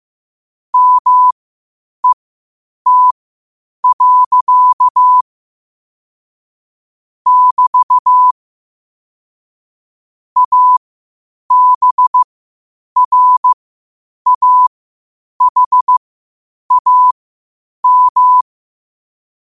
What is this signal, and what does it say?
Morse code, heard as a keyed tone, about 15 wpm, spaced as for 8 wpm, 1000 Hz. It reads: MET. <BT> ABRAHAM
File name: qotdfiles / 8wpm